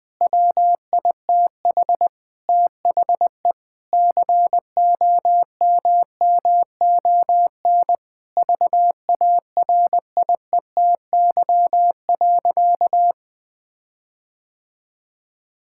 Morse code, heard as a keyed tone, about 20 wpm, 700 Hz. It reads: WITH THE COMMON VARIETY.
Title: Sentences with 6-Word Phrases - 20wpm